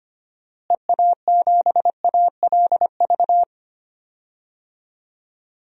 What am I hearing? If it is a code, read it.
Morse code, heard as a keyed tone, about 25 wpm, 700 Hz. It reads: EA7ALV
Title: DXCC-100 Call Signs - 6 Characters - Repeated 3x - 25wpm